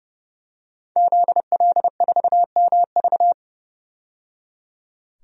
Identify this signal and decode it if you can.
Morse code, heard as a keyed tone, about 30 wpm, 700 Hz. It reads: ZL4MV
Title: DXCC-100 Call Signs - 5 Characters - 30wpm